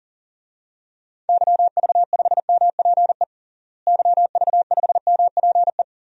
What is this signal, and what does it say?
Morse code, heard as a keyed tone, about 40 wpm, 700 Hz. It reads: YV5MPE YV5MPE
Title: DXCC-100 Call Signs - 6 Characters - Repeated 3x - 40wpm